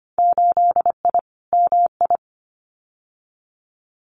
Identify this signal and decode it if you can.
Morse code, heard as a keyed tone, about 25 wpm, 700 Hz. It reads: 8I MI
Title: Winter Field Day Exchanges Encore 25wpm